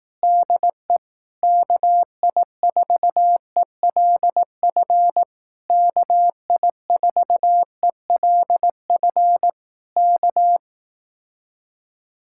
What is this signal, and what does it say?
Morse code, heard as a keyed tone, about 18 wpm, 700 Hz. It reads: DE KI4ELF KI4ELF K
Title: Speed-Racing Common QSO Phrases Encore - 15wpm